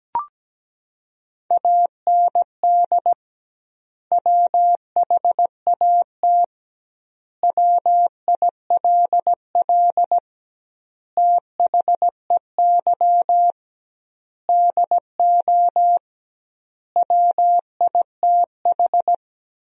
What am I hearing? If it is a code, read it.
Morse code, heard as a keyed tone, about 17 wpm, 700 Hz. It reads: AND WHAT WILL THEY DO WITH
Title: 2x Word-Spacing - Sentences from Top 200 Words - 17wpm